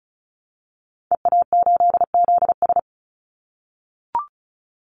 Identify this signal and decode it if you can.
Morse code, heard as a keyed tone, about 36 wpm, 700 Hz. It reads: EA8ZS
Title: Call Signs - 35wpm